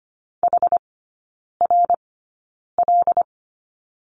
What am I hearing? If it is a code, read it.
Morse code, heard as a keyed tone, about 25 wpm, spaced as for 9 wpm, 690 Hz. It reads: HRL
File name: Lesson 14e - Number 2 - Three Characters 25wpm